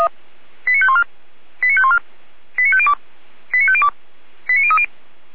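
Five-tone selective call (ZVEI), heard as quick runs of five tones, tones of about 70 ms. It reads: 89514 89514 89502 89502 89030
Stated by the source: ZVEI five-tone selective call 89001 received by SDR, from the feuerwehr-orlamuende fire brigade audio archive.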